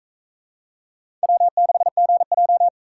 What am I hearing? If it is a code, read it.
Morse code, heard as a keyed tone, about 44 wpm, 700 Hz. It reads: W6GJ